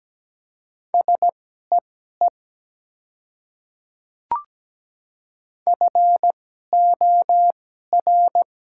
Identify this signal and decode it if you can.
Morse code, heard as a keyed tone, about 17 wpm, spaced as for 12 wpm, 700 Hz. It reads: SEE FOR
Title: 2x Character Spacing - Top 100 Words - 17wpm